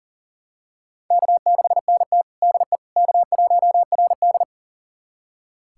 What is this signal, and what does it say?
Morse code, heard as a keyed tone, about 40 wpm, 700 Hz. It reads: K6NT DE K1RD
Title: USA Callsign DE Callsign - 4 Characters - 40wpm